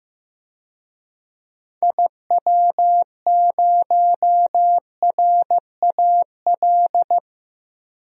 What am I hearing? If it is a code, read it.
Morse code, heard as a keyed tone, about 16 wpm, 700 Hz. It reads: IW0RAL
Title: DXCC-100 Call Signs - 6 Characters - 15wpm